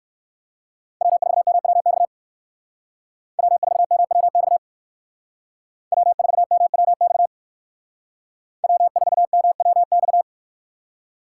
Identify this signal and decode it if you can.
Morse code, heard as a keyed tone, about 59 wpm, 700 Hz. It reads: W4MWX W4MWX W4MWX W4MWX